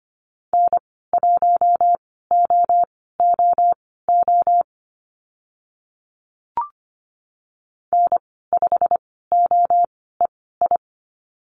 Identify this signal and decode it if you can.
Morse code, heard as a keyed tone, about 25 wpm, spaced as for 16 wpm, 700 Hz. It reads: N1OOO N5OEI
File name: Lesson 15g - Number 5 - Call Signs 25wpm